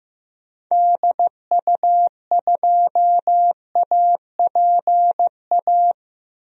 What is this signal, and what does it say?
Morse code, heard as a keyed tone, about 15 wpm, 700 Hz. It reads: DU2APA
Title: DXCC-100 Call Signs - 6 Characters - Repeated 3x - 15wpm